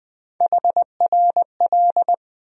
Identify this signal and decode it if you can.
Morse code, heard as a keyed tone, about 20 wpm, 700 Hz. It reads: HRL
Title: USA Call Sign Postfixes - 3 Letters 20wpm